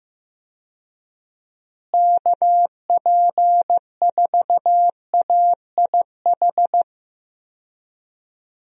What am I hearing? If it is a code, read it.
Morse code, heard as a keyed tone, about 15 wpm, 700 Hz. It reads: KP4AIH
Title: DXCC-100 Call Signs - 6 Characters - Repeated 2x - 15wpm